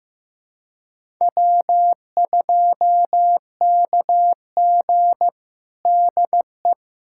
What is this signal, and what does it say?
Morse code, heard as a keyed tone, about 15 wpm, 700 Hz. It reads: W2KG DE
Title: USA Callsign DE Callsign - 4 Characters - 15wpm